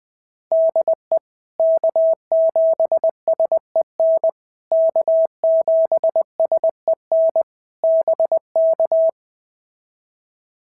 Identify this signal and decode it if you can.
Morse code, heard as a keyed tone, about 19 wpm, 650 Hz. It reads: DE K7SEN K7SEN BK
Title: POTA QSO Practice - 20wpm